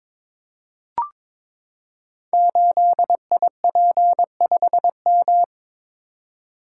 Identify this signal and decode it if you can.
Morse code, heard as a keyed tone, about 22 wpm, 700 Hz. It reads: 8IP5M
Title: Code Groups - 5 Letter:Numbers - 22wpm